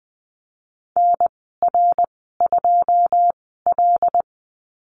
Morse code, heard as a keyed tone, about 20 wpm, spaced as for 14 wpm, 700 Hz. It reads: NR2L
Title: 082 Lesson 14g - Number 2 - Call Signs 20wpm